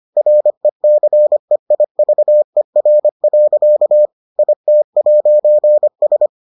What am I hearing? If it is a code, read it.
Morse code, heard as a keyed tone, about 25 wpm, 600 Hz. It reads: RECEIVER. IT'S